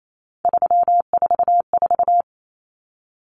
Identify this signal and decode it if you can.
Morse code, heard as a keyed tone, about 28 wpm, 700 Hz. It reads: 344
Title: Numbers - 3-Digit 28wpm